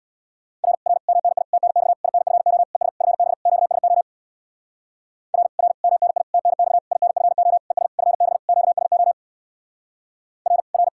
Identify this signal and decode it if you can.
Morse code, heard as a keyed tone, about 61 wpm, 700 Hz. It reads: R R MNI TNX INFO ES FB QSO R R MNI TNX INFO ES FB QSO R R
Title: Speed-Racing Common QSO Phrases - 50wpm